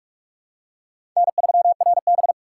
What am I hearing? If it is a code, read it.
Morse code, heard as a keyed tone, about 47 wpm, 700 Hz. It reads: N3RB
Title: USA Call Signs - 4 Characters - 2x Character Spacing - 45wpm